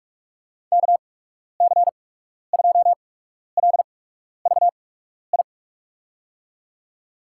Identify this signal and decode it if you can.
Morse code, heard as a keyed tone, about 46 wpm, spaced as for 12 wpm, 700 Hz. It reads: KC2LVI
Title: USA Call Signs - 6 Characters - 8x Character Spacing - 45wpm